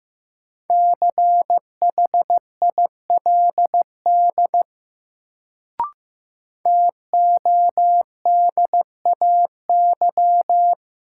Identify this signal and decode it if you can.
Morse code, heard as a keyed tone, about 15 wpm, 700 Hz. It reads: CHILD TODAY
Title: Sets of 5 Words Encore - New Word Drill - 15wpm